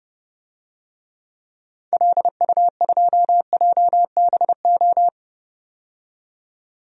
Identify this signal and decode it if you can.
Morse code, heard as a keyed tone, about 31 wpm, 700 Hz. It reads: LU2JBO
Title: DXCC-100 Call Signs - 6 Characters - 30wpm